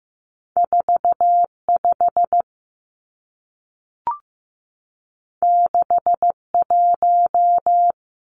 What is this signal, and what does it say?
Morse code, heard as a keyed tone, about 15 wpm, 700 Hz. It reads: 45 61
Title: Numbers - 2-Digit 15wpm